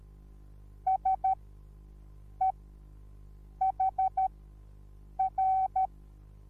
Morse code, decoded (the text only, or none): SEHR